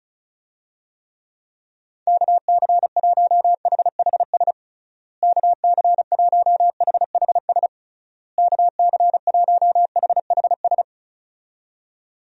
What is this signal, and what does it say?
Morse code, heard as a keyed tone, about 34 wpm, 700 Hz. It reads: KC1HHS KC1HHS KC1HHS
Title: USA Call Signs - 6 Characters - Repeated 3x - 35wpm